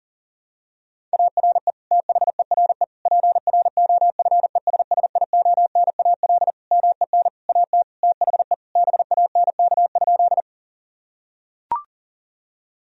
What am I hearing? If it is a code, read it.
Morse code, heard as a keyed tone, about 40 wpm, 700 Hz. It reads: ARE THERE PROFESSIONAL MEN AT THE BANK?
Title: Sentences from Top 1000 Words - Review Part 4 - 40wpm